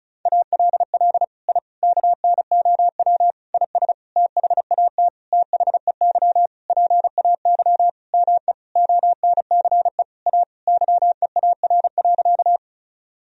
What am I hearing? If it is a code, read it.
Morse code, heard as a keyed tone, about 34 wpm, 700 Hz. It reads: ALL I KNOW IS THAT THEY PAY ME ONCE A YEAR.
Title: Sentences from Top 1000 Words - Review Part 1 - 35wpm